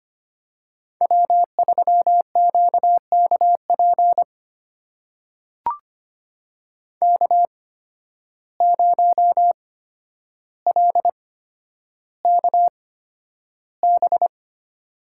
Morse code, heard as a keyed tone, about 25 wpm, spaced as for 11 wpm, 700 Hz. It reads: W3QKP K0LKB
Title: USA Call Signs - 5 Characters - 8x Character Spacing - 25wpm